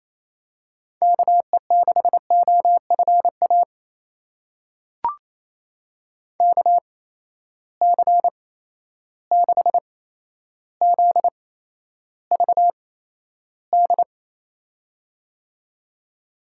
Morse code, heard as a keyed tone, about 28 wpm, spaced as for 12 wpm, 700 Hz. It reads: KE6OFA KC6ZVD
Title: USA Call Signs - 6 Characters - 8x Character Spacing - 28wpm